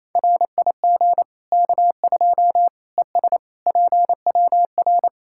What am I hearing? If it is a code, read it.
Morse code, heard as a keyed tone, about 28 wpm, 700 Hz. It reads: RIG K2 ES PWR
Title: Practice QSO - Exchange 2 - Info - 28wpm